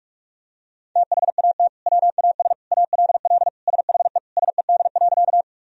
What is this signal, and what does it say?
Morse code, heard as a keyed tone, about 41 wpm, 700 Hz. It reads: THAT WAS ALL SHE SED.
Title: Sentences from Top 100 Words and Abbreviations 45wpm